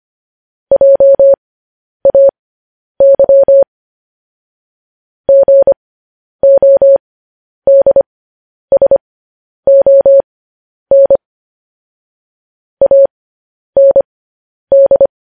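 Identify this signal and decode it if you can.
Morse code, heard as a keyed tone, about 25 wpm, spaced as for 10 wpm, 550 Hz. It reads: JAY GODSON AND
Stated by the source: Morse code transmission of news headlines.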